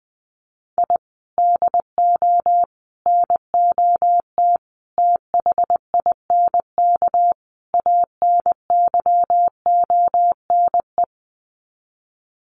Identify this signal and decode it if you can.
Morse code, heard as a keyed tone, about 20 wpm, 700 Hz. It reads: I DO NOT THINK ANYONE
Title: Sets of 5 Words Encore 20wpm